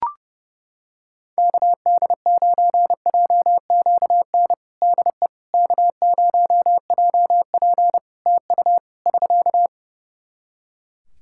Morse code, heard as a keyed tone, about 30 wpm, 700 Hz. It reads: KD9JQN DE K0JP TU <SK>
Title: Practice QSO - Exchange 4 - Ending - 30wpm